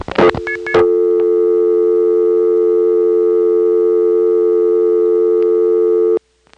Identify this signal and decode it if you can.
Dial tone (North America), 350 + 440 Hz.